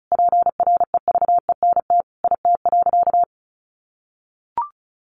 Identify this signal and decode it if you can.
Morse code, heard as a keyed tone, about 34 wpm, 700 Hz. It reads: PREVENT IT.
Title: Sentences from Top 1100 Words 35wpm